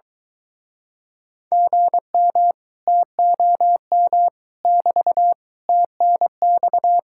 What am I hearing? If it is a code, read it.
Morse code, heard as a keyed tone, about 24 wpm, 700 Hz. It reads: GM TOM <BT> TNX